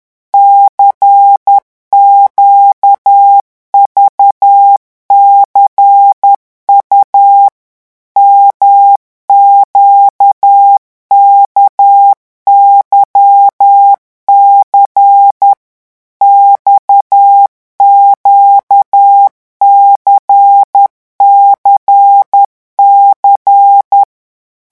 Morse code, heard as a keyed tone, about 10 wpm, 800 Hz. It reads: CQVCU MQKYC XQCCC
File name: morse mp3 / revision2-vitesse 10 mots